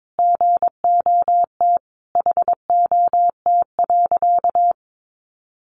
Morse code, heard as a keyed tone, about 22 wpm, 700 Hz. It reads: GOT HOT.
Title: Sentences from Top 1200 Words 22wpm